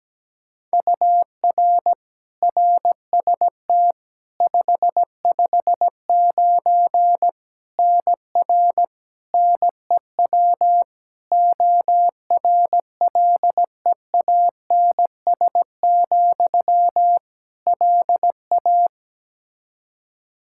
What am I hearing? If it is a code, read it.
Morse code, heard as a keyed tone, about 17 wpm, 700 Hz. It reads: UR RST 559 NR NEW ORLEANS, LA
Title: K1USN SST QSO Practice 17wpm